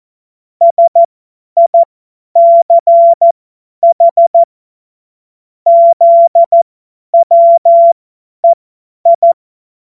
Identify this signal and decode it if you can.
Morse code, heard as a keyed tone, about 14 wpm, spaced as for 10 wpm, 680 Hz. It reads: SICH ZWEI
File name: Witz-08-Juli-2025-50-F70